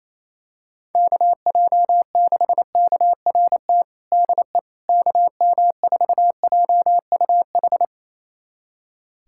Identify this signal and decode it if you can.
Morse code, heard as a keyed tone, about 28 wpm, 700 Hz. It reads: KJ6KRT DE KM4JUH
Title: USA Callsign DE Callsign 28wpm